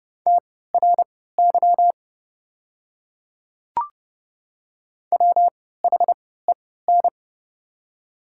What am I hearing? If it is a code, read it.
Morse code, heard as a keyed tone, about 30 wpm, spaced as for 17 wpm, 700 Hz. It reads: TRY WHEN
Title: 3x Character Spacing - Top 200 Words - 30wpm